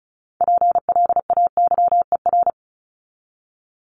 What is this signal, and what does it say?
Morse code, heard as a keyed tone, about 34 wpm, 700 Hz. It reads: PLAYER
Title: Top 501-750 Words 35wpm